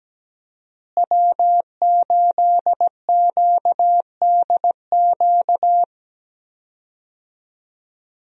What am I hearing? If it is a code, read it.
Morse code, heard as a keyed tone, about 17 wpm, 700 Hz. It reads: W8QDQ
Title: USA Call Signs - 5 Characters - Repeated 2x - 17wpm